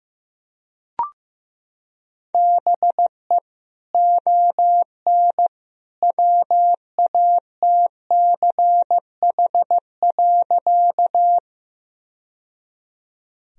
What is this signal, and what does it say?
Morse code, heard as a keyed tone, about 15 wpm, 700 Hz. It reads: BE ON WATCH.
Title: Sentences from Top 1000 Words - Review Part 1 - 15wpm